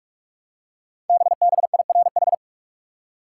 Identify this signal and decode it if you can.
Morse code, heard as a keyed tone, about 44 wpm, 700 Hz. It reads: BBIRH
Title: Code Groups - 5 Letters - 45wpm